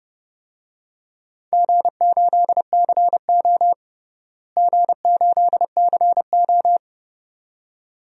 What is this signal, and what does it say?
Morse code, heard as a keyed tone, about 30 wpm, 700 Hz. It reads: G8CO G8CO
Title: DXCC-100 Call Signs - 4 Characters - Repeated 2x - 30wpm